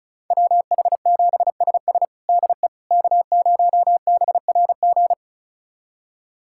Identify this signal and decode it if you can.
Morse code, heard as a keyed tone, about 36 wpm, 700 Hz. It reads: WH7SS DE K0BRG
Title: USA Callsign DE Callsign - 5 Characters - 35wpm